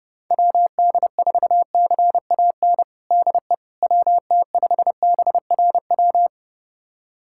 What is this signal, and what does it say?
Morse code, heard as a keyed tone, about 30 wpm, 700 Hz. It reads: WD4CAN DE WT5BRW
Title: USA Callsign DE Callsign 30wpm